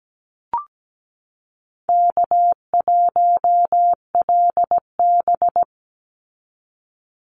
Morse code, heard as a keyed tone, about 17 wpm, 700 Hz. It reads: K1LB
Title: NA SOTA Chaser Callsigns 17wpm